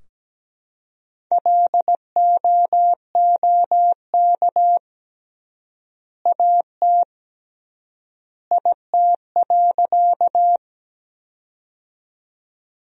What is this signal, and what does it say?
Morse code, heard as a keyed tone, about 17 wpm, 700 Hz. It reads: LOOK AT IT.
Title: 3x Word-Spacing - Sentences from Top 100 Words - 17wpm